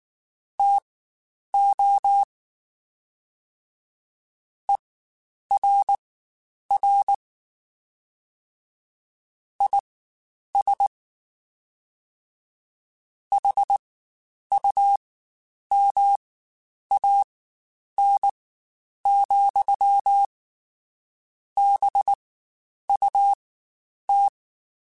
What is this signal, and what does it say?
Morse code, heard as a keyed tone, about 19 wpm, spaced as for 9 wpm, 800 Hz. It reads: TO ERR IS HUMAN, BUT
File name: Spaced / Chuckles Spaced-mp3s / HumanErr sp @18